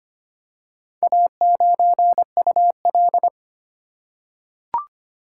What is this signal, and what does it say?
Morse code, heard as a keyed tone, about 25 wpm, 700 Hz. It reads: A9UL
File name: practice / 25wpm / Four Letter-Number Combinations - 25wpm